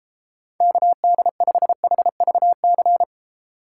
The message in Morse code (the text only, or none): KD5HVC